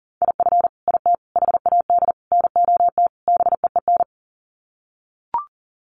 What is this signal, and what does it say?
Morse code, heard as a keyed tone, about 40 wpm, 700 Hz. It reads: IF IT HAD NOT BEEN
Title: Sets of 5 Words Encore 40wpm